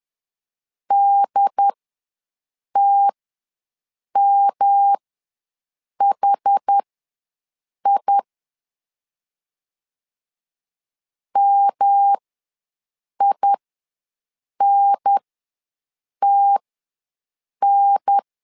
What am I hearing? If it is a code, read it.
Morse code, heard as a keyed tone, about 10 wpm, spaced as for 6 wpm, 800 Hz. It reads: DTMHI MINTN
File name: LEZIONI / 3 MWRO / 3 M ws 10wpm 2121